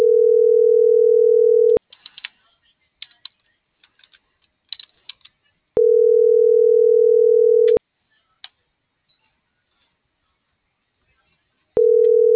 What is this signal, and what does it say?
Ringback tone (North America), 440 + 480 Hz.